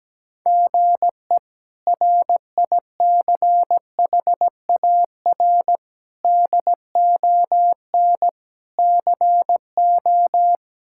Morse code, heard as a keyed tone, about 17 wpm, 700 Hz. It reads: GE RICHAR DON CO
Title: K1USN SST QSO Practice - 17wpm